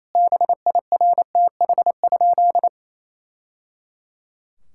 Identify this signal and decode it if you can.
Morse code, heard as a keyed tone, about 28 wpm, 700 Hz. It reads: BIRTH?
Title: Sentences from Top 1000 Words - Review Part 5 - 28wpm